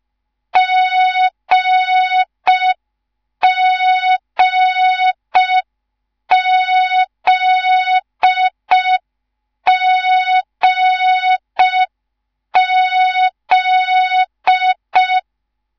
Morse code, heard as a keyed tone, about 5 wpm, 3700 Hz. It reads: GGZGZ
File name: Morze-HA2MN / hang / 10GWZ HA2MN